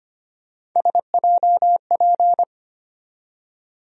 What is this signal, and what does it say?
Morse code, heard as a keyed tone, about 25 wpm, 700 Hz. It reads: SJP
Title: USA Call Sign Postfixes - 3 Letters 25wpm